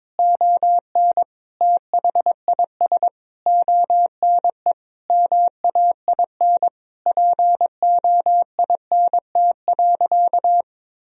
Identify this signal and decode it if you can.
Morse code, heard as a keyed tone, about 22 wpm, 700 Hz. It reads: ON THIS ONE MAIN POINT.
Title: Sentences from Top 1000 Words - Review Part 2 - 22wpm